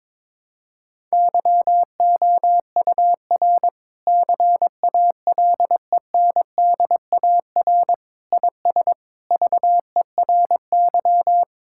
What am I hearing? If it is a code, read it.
Morse code, heard as a keyed tone, about 22 wpm, 700 Hz. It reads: YOUR CALENDAR IS VERY
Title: Sentences from Top 1600 Words 22wpm